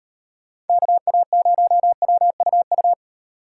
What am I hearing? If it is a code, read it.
Morse code, heard as a keyed tone, about 36 wpm, 700 Hz. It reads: KA0WUU